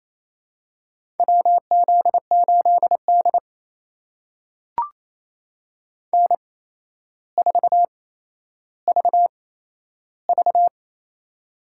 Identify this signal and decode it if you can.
Morse code, heard as a keyed tone, about 28 wpm, spaced as for 12 wpm, 700 Hz. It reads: WZ8D N4VV
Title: USA Call Signs - 4 Characters - 8x Character Spacing - 28wpm